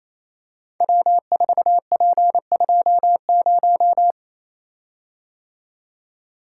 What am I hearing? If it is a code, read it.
Morse code, heard as a keyed tone, about 28 wpm, 700 Hz. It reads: W4P20
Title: Code Groups - 5 Letter:Numbers - 28wpm